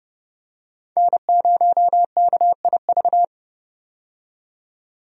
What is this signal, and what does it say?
Morse code, heard as a keyed tone, about 30 wpm, 700 Hz. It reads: N0KIV
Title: USA Call Signs - 5 Characters - 30wpm